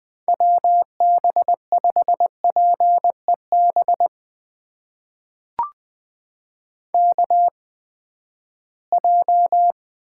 Morse code, heard as a keyed tone, about 20 wpm, spaced as for 14 wpm, 700 Hz. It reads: WB5PEB KJ
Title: USA Call Signs - 8x Character Spacing - 20wpm